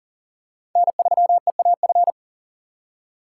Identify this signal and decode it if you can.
Morse code, heard as a keyed tone, about 40 wpm, 700 Hz. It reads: N3EAF
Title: USA Call Signs - 5 Characters - 6x Character Spacing - 40wpm